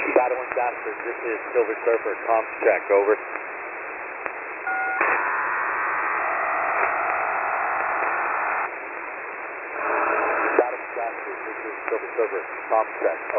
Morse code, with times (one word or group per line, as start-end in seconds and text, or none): none